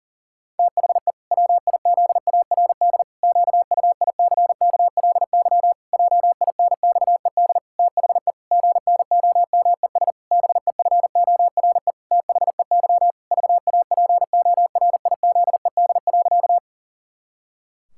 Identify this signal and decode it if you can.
Morse code, heard as a keyed tone, about 40 wpm, 700 Hz. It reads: THE WIZARD QUICKLY JINXED THE GNOMES BEFORE THEY VAPORIZED.
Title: Pangram Challenge v2 - 40wpm